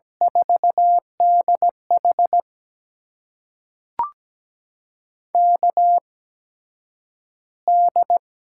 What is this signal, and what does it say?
Morse code, heard as a keyed tone, about 17 wpm, spaced as for 9 wpm, 700 Hz. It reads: 4DH KD
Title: USA Call Signs - 8x Character Spacing - 17wpm